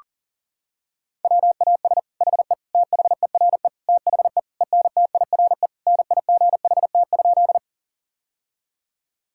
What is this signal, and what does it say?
Morse code, heard as a keyed tone, about 40 wpm, 700 Hz. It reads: WAS HE THERE THE ENTIRE NIGHT?
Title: Sentences from Top 1000 Words - Review Part 5 - 40wpm